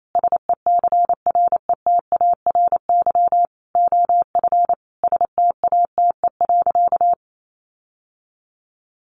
Morse code, heard as a keyed tone, about 28 wpm, 700 Hz. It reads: SECRETARY OF STATE.
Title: Sentences from Top 1400 Words 28wpm